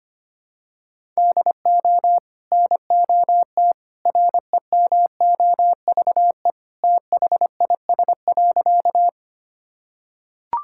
Morse code, heard as a keyed tone, about 25 wpm, 700 Hz. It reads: DO NOT REMOVE THIS.
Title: Sentences from Top 1000 Words - Review Part 5 - 25wpm